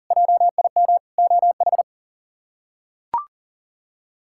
Morse code, heard as a keyed tone, about 40 wpm, 700 Hz. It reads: JIM OH
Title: K1USN SST QSO Practice - 40wpm